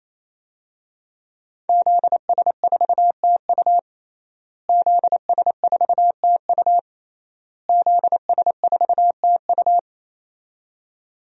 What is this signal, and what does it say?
Morse code, heard as a keyed tone, about 27 wpm, 700 Hz. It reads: ZS4TU ZS4TU ZS4TU
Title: DXCC-100 Call Signs - 5 Characters - Repeated 3x - 28wpm